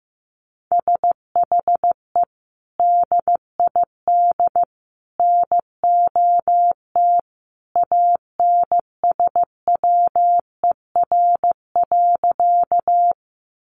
Morse code, heard as a keyed tone, about 15 wpm, 700 Hz. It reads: SHE DID NOT ANSWER.
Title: Sentences from Top 600 Words 15wpm